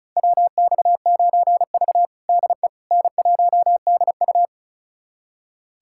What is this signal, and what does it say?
Morse code, heard as a keyed tone, about 36 wpm, 700 Hz. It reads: WX9V DE N1DU
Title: USA Callsign DE Callsign - 4 Characters - 35wpm